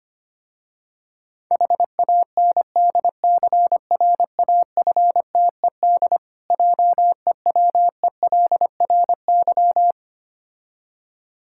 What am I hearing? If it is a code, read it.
Morse code, heard as a keyed tone, about 25 wpm, 700 Hz. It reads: HANDCRAFTED JEWELRY
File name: practice / 25wpm / Difficult Words with Hints - 25wpm